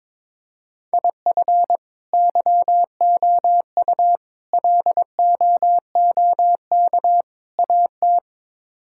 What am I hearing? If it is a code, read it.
Morse code, heard as a keyed tone, about 22 wpm, 700 Hz. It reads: IF YOU LOOK AT
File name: practice / 22wpm / Build A Phrase - 5-Word Phrases - 22wpm